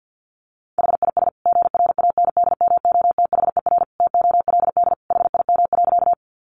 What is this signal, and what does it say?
Morse code, heard as a keyed tone, about 50 wpm, 700 Hz. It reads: HIS GRANDMOTHER TOLD HIM.
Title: Sentences from Top 1200 Words 50wpm